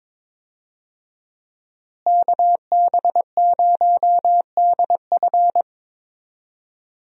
Morse code, heard as a keyed tone, about 22 wpm, 700 Hz. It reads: KB0DF
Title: USA Call Signs - 5 Characters - Repeated 2x - 22wpm